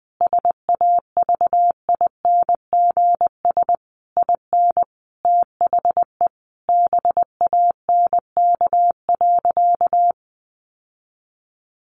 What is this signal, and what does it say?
Morse code, heard as a keyed tone, about 20 wpm, 700 Hz. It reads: SAVINGS IN THE BANK.
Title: Sentences from Top 1200 Words 20wpm